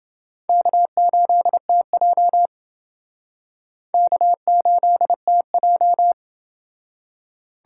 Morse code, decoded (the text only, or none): K8TJ K8TJ